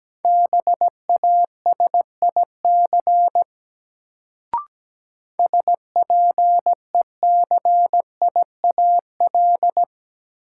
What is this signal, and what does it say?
Morse code, heard as a keyed tone, about 17 wpm, 700 Hz. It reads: BASIC SPECIAL
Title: Sentences from Top 1000 Words - Review Part 3 - New Word Drill - 17wpm